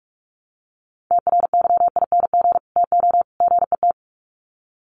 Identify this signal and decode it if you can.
Morse code, heard as a keyed tone, about 44 wpm, 700 Hz. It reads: TRYING TO GET
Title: Sets of 3 Words 45wpm